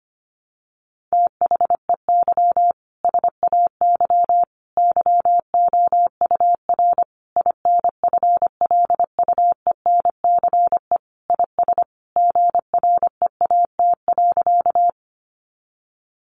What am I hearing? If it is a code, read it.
Morse code, heard as a keyed tone, about 25 wpm, 700 Hz. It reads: THEY SAY YOUR INFLUENCE IS GREAT.
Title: Sentences from Top 1100 Words 25wpm